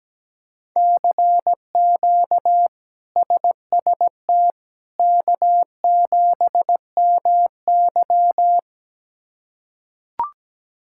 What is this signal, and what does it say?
Morse code, heard as a keyed tone, about 17 wpm, 700 Hz. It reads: CQ SST K7MY
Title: K1USN SST QSO Practice - 17wpm